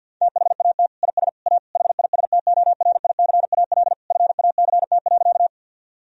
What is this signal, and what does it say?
Morse code, heard as a keyed tone, about 51 wpm, 700 Hz. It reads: THAT IS A HISTORICAL FACT.
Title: Sentences from Top 1000 Words - Review Part 4 - 50wpm